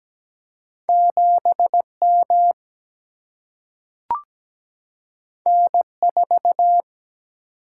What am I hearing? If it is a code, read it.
Morse code, heard as a keyed tone, about 17 wpm, 700 Hz. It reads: 7 M N 4
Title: Single-Letter Single-Number 17wpm